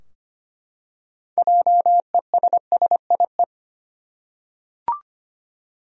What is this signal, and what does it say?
Morse code, heard as a keyed tone, about 25 wpm, 700 Hz. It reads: JESSIE
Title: USA Common Names - 25wpm